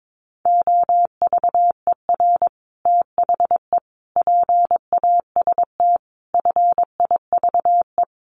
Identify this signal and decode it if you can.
Morse code, heard as a keyed tone, about 22 wpm, 700 Hz. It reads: OVER THE PAST FIVE
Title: Sets of 5 Words Encore 22wpm